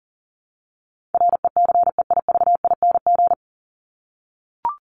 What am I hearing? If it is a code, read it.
Morse code, heard as a keyed tone, about 40 wpm, 700 Hz. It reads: RECEIVING
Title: ING Suffix - Top 100 Words with Base Words 40wpm